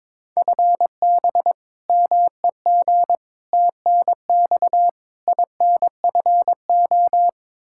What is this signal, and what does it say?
Morse code, heard as a keyed tone, about 22 wpm, 700 Hz. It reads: FB MEG TNX INFO
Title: Practice QSO - Exchange 3 - More Info - 22wpm